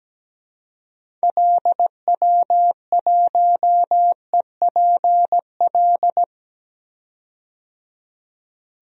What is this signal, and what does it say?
Morse code, heard as a keyed tone, about 17 wpm, 700 Hz. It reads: LW1EPL
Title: DXCC-100 Call Signs - 6 Characters - 17wpm